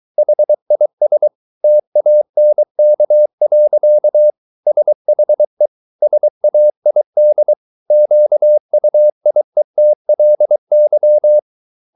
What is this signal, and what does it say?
Morse code, heard as a keyed tone, about 23 wpm, 600 Hz. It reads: HIS TANK. SHE SAID QUIETLY